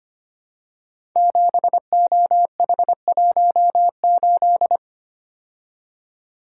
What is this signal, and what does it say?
Morse code, heard as a keyed tone, about 25 wpm, 700 Hz. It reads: 7OH18
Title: Code Groups - 5 Letter:Numbers - 25wpm